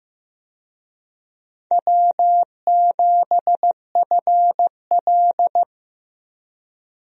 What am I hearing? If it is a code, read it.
Morse code, heard as a keyed tone, about 15 wpm, 700 Hz. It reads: W7FL
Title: USA Call Signs - 4 Characters - Repeated 2x - 15wpm